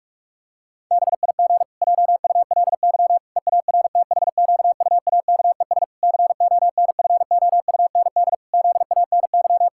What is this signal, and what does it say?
Morse code, heard as a keyed tone, about 46 wpm, 700 Hz. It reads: BIG JULY EARTHQUAKES CONFOUND ZANY
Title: Pangram Challenge v2 - 45wpm